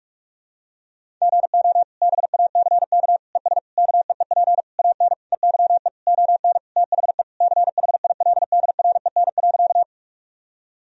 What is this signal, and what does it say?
Morse code, heard as a keyed tone, about 41 wpm, 700 Hz. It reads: GO BACK ES KEEP AN EYE ON THE CHILDREN.
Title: Sentences from Top 300 Words and Abbreviations 45wpm